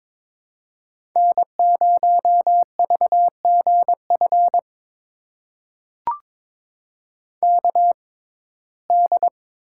Morse code, heard as a keyed tone, about 21 wpm, spaced as for 16 wpm, 700 Hz. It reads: N0VGF KD
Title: USA Call Signs - 6x Character Spacing - 22wpm